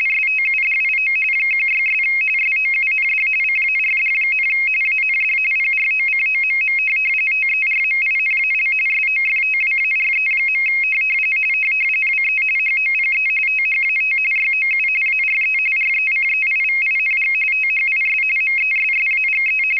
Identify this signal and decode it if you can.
RTTY, about 45.45 baud. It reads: STUDIO 16' SOUND CARD, AND PROCOMM TERMINAL SOFTWARE. WE HOPE THAT YOU ARE RECEIVING THIS MESSAGE
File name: rtty425 / rtty45